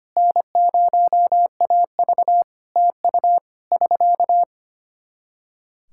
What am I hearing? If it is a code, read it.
Morse code, heard as a keyed tone, about 25 wpm, 700 Hz. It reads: N0AV TU <SK>
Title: K1USN SST QSO Practice 25wpm